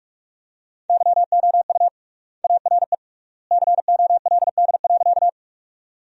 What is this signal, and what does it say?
Morse code, heard as a keyed tone, about 46 wpm, 700 Hz. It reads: YOU ARE COLD.
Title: 3x Word-Spacing - Sentences from Top 400 Words - 45wpm